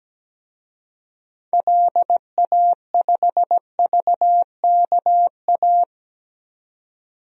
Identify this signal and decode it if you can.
Morse code, heard as a keyed tone, about 17 wpm, 700 Hz. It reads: LA5VKA
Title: DXCC-100 Call Signs - 6 Characters - 17wpm